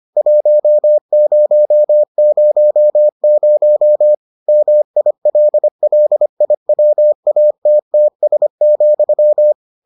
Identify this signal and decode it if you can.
Morse code, heard as a keyed tone, about 25 wpm, 600 Hz. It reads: 1000 MILLIWATTS,